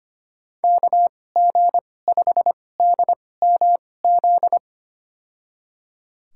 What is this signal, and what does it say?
Morse code, heard as a keyed tone, about 25 wpm, spaced as for 18 wpm, 700 Hz. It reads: KG5DMZ
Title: USA Call Signs - 6 Characters - 2x Character Spacing - 25wpm